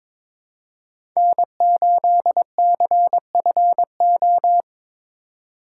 Morse code, heard as a keyed tone, about 21 wpm, 700 Hz. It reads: N8CFO
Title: USA Call Signs - 5 Characters - 3x Character Spacing - 22wpm